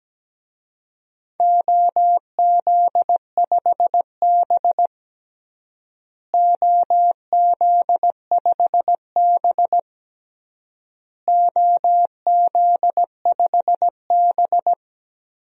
Morse code, heard as a keyed tone, about 17 wpm, 700 Hz. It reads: OZ5B OZ5B OZ5B
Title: DXCC-100 Call Signs - 4 Characters - Repeated 3x - 17wpm